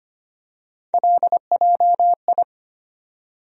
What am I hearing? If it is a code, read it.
Morse code, heard as a keyed tone, about 25 wpm, 700 Hz. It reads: LJI
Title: USA Call Sign Postfixes - 3 Letters 25wpm